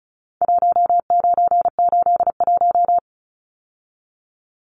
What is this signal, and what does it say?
Morse code, heard as a keyed tone, about 34 wpm, 700 Hz. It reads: 1981
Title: Numbers - Contest Style - 4-Digit 35wpm